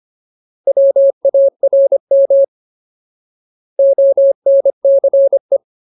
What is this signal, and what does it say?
Morse code, heard as a keyed tone, about 25 wpm, 550 Hz. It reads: WARM ONCE
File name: Ultimate Word List / 25WPM Ultimate